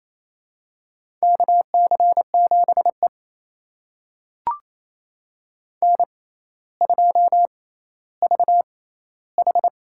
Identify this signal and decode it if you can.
Morse code, heard as a keyed tone, about 28 wpm, spaced as for 14 wpm, 700 Hz. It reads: KC7E N2VH
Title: USA Call Signs - 4 Characters - 6x Character Spacing - 28wpm